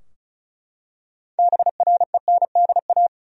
Morse code, heard as a keyed tone, about 36 wpm, 700 Hz. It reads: BRENDA
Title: USA Common Names - 35wpm